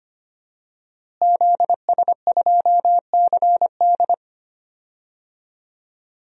Morse code, heard as a keyed tone, about 25 wpm, 700 Hz. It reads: ZS2CD
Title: DXCC-100 Call Signs - 5 Characters - Repeated 2x - 25wpm